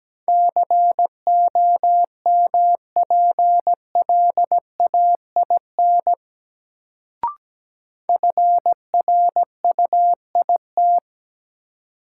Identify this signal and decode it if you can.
Morse code, heard as a keyed tone, about 17 wpm, 700 Hz. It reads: COMPLAIN FRUIT
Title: Sentences from Top 1300 Words - New Word Drill - 17wpm